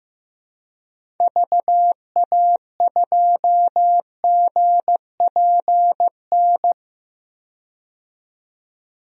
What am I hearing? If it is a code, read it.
Morse code, heard as a keyed tone, about 15 wpm, 700 Hz. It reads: VA2GPN
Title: DXCC-100 Call Signs - 6 Characters - 15wpm